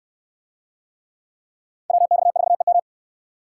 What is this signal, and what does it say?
Morse code, heard as a keyed tone, about 66 wpm, 700 Hz. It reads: WX4EM